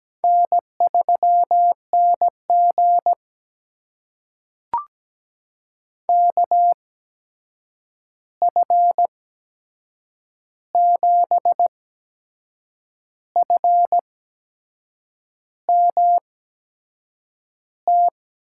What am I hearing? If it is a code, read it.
Morse code, heard as a keyed tone, about 17 wpm, spaced as for 6 wpm, 700 Hz. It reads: N3NG KF7FMT
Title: USA Call Signs - 8x Character Spacing - 17wpm